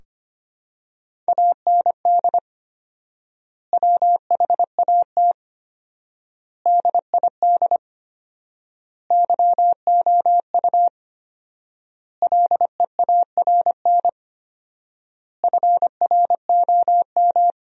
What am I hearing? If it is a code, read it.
Morse code, heard as a keyed tone, about 25 wpm, 700 Hz. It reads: AND WHAT DID YOU LEARN FROM
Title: 4x Word-Spacing - Sentences from Top 300 Words - 25wpm